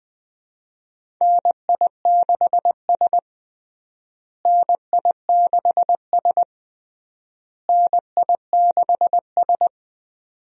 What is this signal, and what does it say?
Morse code, heard as a keyed tone, about 20 wpm, 700 Hz. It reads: NI6S NI6S NI6S
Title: USA Call Signs - 4 Characters - Repeated 3x - 20wpm